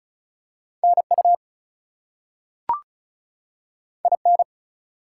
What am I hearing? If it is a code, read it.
Morse code, heard as a keyed tone, about 36 wpm, 700 Hz. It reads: NU IN
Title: US States and Canadian Provinces 35wpm